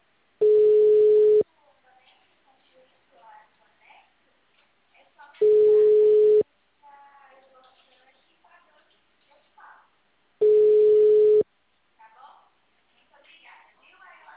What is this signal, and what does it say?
Ringback tone (Europe, 425 Hz), with speech.
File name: out- -2001- - - .25